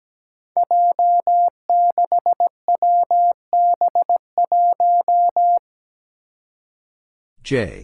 Morse code, heard as a keyed tone, about 17 wpm, 700 Hz. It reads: J6WB1
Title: Five Challenging Letter-Number Combinations 17wpm